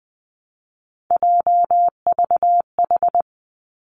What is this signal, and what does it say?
Morse code, heard as a keyed tone, about 20 wpm, 700 Hz. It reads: JVH
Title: Three Challenging Letter-Number Combinations 20wpm